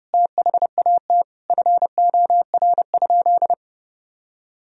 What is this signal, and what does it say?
Morse code, heard as a keyed tone, about 32 wpm, 700 Hz. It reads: THAT FOR?
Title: Sentences from Top 1000 Words - Review Part 4 - 30wpm